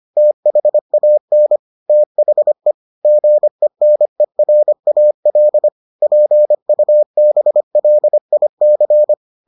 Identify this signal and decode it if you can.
Morse code, heard as a keyed tone, about 25 wpm, 600 Hz. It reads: THAN THE GENERAL PUBLIC